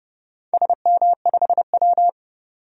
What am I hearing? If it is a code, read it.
Morse code, heard as a keyed tone, about 30 wpm, 700 Hz. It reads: SM5W
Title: DXCC-100 Call Signs - 4 Characters - 30wpm